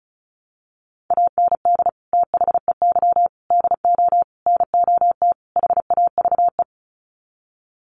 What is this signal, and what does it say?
Morse code, heard as a keyed tone, about 34 wpm, 700 Hz. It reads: AND THEY DO NOT HAVE
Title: Sets of 5 Words Encore2 35wpm